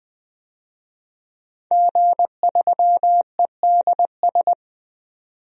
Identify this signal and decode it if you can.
Morse code, heard as a keyed tone, about 20 wpm, 700 Hz. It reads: G3EDS
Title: DXCC-100 Call Signs - 5 Characters - 20wpm